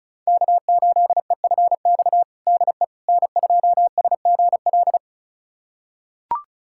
Morse code, heard as a keyed tone, about 36 wpm, 700 Hz. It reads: K8EFX DE N2SGL
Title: USA Callsign DE Callsign - 5 Characters - 35wpm